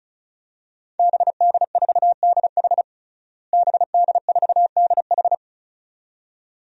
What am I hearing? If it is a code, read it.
Morse code, heard as a keyed tone, about 36 wpm, 700 Hz. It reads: BD4DH BD4DH
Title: DXCC-100 Call Signs - 5 Characters - Repeated 2x - 35wpm